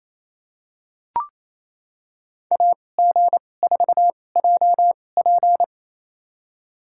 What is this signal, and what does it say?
Morse code, heard as a keyed tone, about 28 wpm, spaced as for 20 wpm, 700 Hz. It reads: AG4JP
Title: USA Call Signs - 5 Characters - 2x Character Spacing - 28wpm